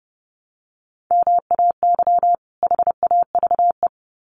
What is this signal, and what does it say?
Morse code, heard as a keyed tone, about 30 wpm, 700 Hz. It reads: MAY HAVE
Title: Sets of 2 Words 30wpm